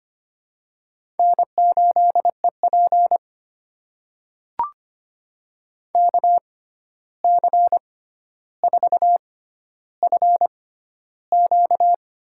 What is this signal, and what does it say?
Morse code, heard as a keyed tone, about 25 wpm, spaced as for 12 wpm, 700 Hz. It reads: N8EP KC4FQ
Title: USA Call Signs - 6x Character Spacing - 25wpm